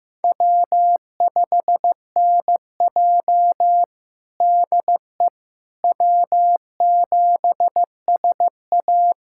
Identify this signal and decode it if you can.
Morse code, heard as a keyed tone, about 15 wpm, 700 Hz. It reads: W5NJ DE W7SA
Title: USA Callsign DE Callsign - 4 Characters - 15wpm